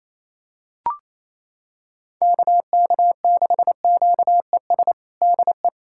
Morse code, heard as a keyed tone, about 28 wpm, 700 Hz. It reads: KK6QES DE
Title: USA Callsign DE Callsign - 6 Characters - 28wpm